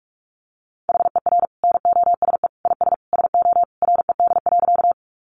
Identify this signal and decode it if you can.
Morse code, heard as a keyed tone, about 44 wpm, 700 Hz. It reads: HER NOSE IS SO RED.
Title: Easy Phrases 45wpm